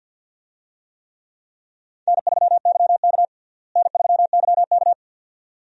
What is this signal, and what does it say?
Morse code, heard as a keyed tone, about 48 wpm, 700 Hz. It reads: N3YK N3YK
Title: USA Call Signs - 4 Characters - Repeated 2x - 50wpm